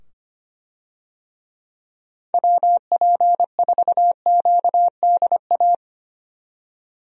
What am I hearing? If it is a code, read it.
Morse code, heard as a keyed tone, about 25 wpm, 700 Hz. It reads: WP4QDA
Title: USA Call Signs - 6 Characters - 25wpm